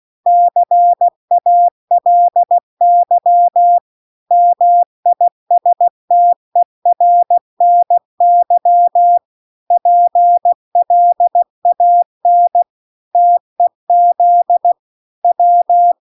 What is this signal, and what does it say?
Morse code, heard as a keyed tone, about 16 wpm, 700 Hz. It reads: CALY MISTERNY PLAN TEZ W